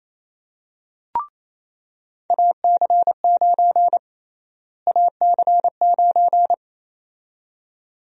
Morse code, heard as a keyed tone, about 28 wpm, 700 Hz. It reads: AC9 AC9
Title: USA Call Sign Prefixes - Repeated 2x - 28wpm